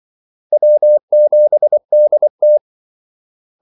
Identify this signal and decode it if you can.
Morse code, heard as a keyed tone, about 24 wpm, 600 Hz. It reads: W7DT
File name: Callsigns-24wpm